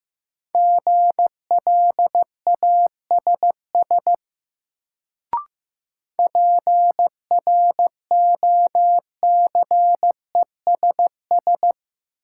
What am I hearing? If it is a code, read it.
Morse code, heard as a keyed tone, about 15 wpm, 700 Hz. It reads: GLASS PROCESS
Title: Sentences from Top 1000 Words - Review Part 3 - New Word Drill - 15wpm